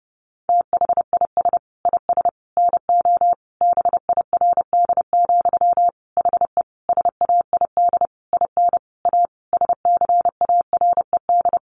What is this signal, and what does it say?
Morse code, heard as a keyed tone, about 28 wpm, 700 Hz. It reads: THIS IS NO BIRD, HE SAID IN A SCARED
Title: Sentences from Top 700 Words 30wpm